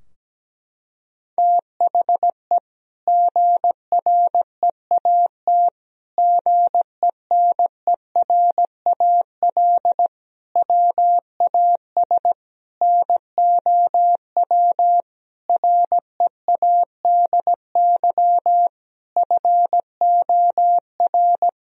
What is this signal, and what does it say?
Morse code, heard as a keyed tone, about 17 wpm, 700 Hz. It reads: THE GREAT GENERAL WAS NOW READY FOR
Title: Sentences from Top 1000 Words - Review Part 4 - 17wpm